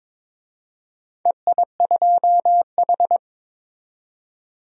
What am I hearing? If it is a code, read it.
Morse code, heard as a keyed tone, about 22 wpm, 700 Hz. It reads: EI2H
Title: DXCC-100 Call Signs - 4 Characters - 22wpm